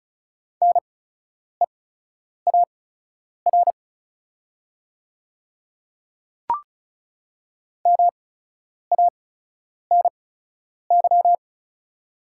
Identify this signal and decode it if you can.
Morse code, heard as a keyed tone, about 36 wpm, spaced as for 10 wpm, 690 Hz. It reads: NEAR MANY
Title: 8x Character Spacing - Top 200 Words - 35wpm